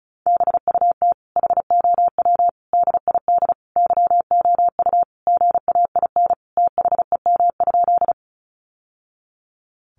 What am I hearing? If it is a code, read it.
Morse code, heard as a keyed tone, about 34 wpm, 700 Hz. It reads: BUT HOW DID YOU GAIN THEM?
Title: Sentences from Top 1000 Words 35wpm